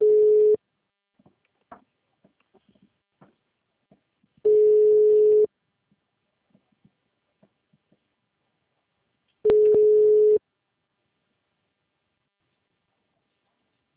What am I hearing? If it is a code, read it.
Ringback tone (Europe, 425 Hz).
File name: out- -2005- - - .22056